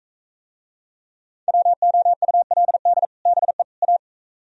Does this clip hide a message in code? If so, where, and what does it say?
Yes: Morse code from 1.5 to 4.0 s, heard as a keyed tone, about 44 wpm, 700 Hz. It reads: WOULD BE A